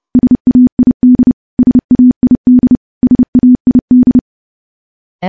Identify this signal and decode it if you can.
Morse code, heard as a keyed tone, about 28 wpm, 250 Hz. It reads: SAID SAID SAID
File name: morse / morse-training-30WPM